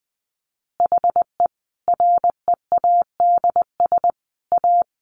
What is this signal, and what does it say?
Morse code, heard as a keyed tone, about 19 wpm, 700 Hz. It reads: HE READS A
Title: Additional Easy Phrases 20wpm